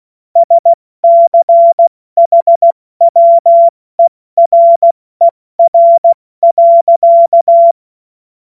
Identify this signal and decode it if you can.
Morse code, heard as a keyed tone, about 16 wpm, 680 Hz. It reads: SCHWERER.
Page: cw-witze